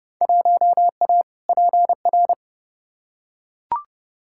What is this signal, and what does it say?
Morse code, heard as a keyed tone, about 30 wpm, 700 Hz. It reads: 1A PR
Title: Field Day Exchanges Encore - 30wpm